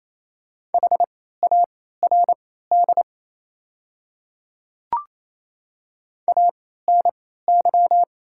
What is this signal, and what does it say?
Morse code, heard as a keyed tone, about 28 wpm, spaced as for 16 wpm, 700 Hz. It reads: HARD ANY
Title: 3x Character Spacing - Top 300 Words - 28wpm